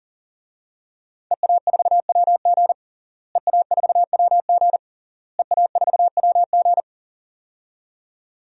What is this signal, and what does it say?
Morse code, heard as a keyed tone, about 40 wpm, 700 Hz. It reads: EA4WG EA4WG EA4WG
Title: DXCC-100 Call Signs - 5 Characters - Repeated 3x - 40wpm